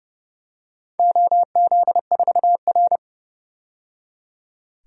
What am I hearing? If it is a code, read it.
Morse code, heard as a keyed tone, about 31 wpm, 700 Hz. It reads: OZ4R
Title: DXCC-100 Call Signs - 4 Characters - 30wpm